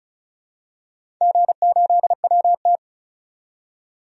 Morse code, heard as a keyed tone, about 36 wpm, 700 Hz. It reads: G8WT
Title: DXCC-100 Call Signs - 4 Characters - 35wpm